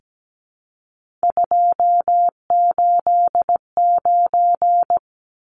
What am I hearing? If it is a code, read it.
Morse code, heard as a keyed tone, about 17 wpm, 700 Hz. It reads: 289
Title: Numbers - 3-Digit 17wpm